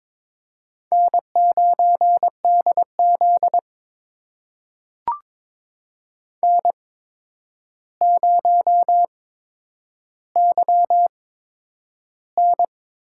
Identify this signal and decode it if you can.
Morse code, heard as a keyed tone, about 22 wpm, spaced as for 9 wpm, 700 Hz. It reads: N9DZ N0YN
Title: USA Call Signs - 4 Characters - 8x Character Spacing - 22wpm